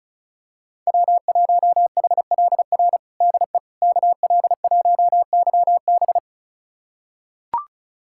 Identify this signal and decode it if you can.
Morse code, heard as a keyed tone, about 34 wpm, 700 Hz. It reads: W1HLR DE KL1YB
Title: USA Callsign DE Callsign - 5 Characters - 35wpm